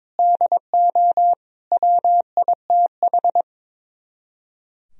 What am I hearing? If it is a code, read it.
Morse code, heard as a keyed tone, about 22 wpm, 700 Hz. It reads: DO WITH
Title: Speed-Racing - Sets of 5 Words 22wpm